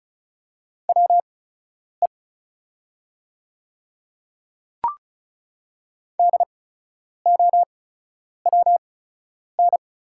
Morse code, heard as a keyed tone, about 36 wpm, spaced as for 10 wpm, 690 Hz. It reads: WE DOWN
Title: 8x Character Spacing - Top 200 Words - 35wpm